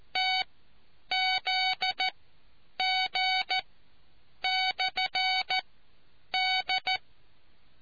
Morse code, heard as a keyed tone, about 14 wpm, spaced as for 8 wpm, 750 Hz. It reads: TZG/D